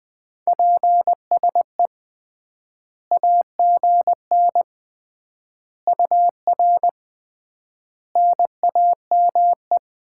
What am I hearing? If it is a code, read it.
Morse code, heard as a keyed tone, about 20 wpm, 700 Hz. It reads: PSE AGN UR NAME
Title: Common QSO Phrases - 3x Word-Spacing - 20wpm